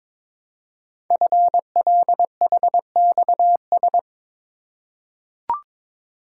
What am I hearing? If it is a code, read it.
Morse code, heard as a keyed tone, about 22 wpm, 700 Hz. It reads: FLHXS
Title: Code Groups - 5 Letters - 22wpm